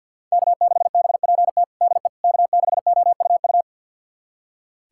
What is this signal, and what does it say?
Morse code, heard as a keyed tone, about 46 wpm, 700 Hz. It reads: K6BPT DE K6OUU
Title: USA Callsign DE Callsign - 5 Characters - 50wpm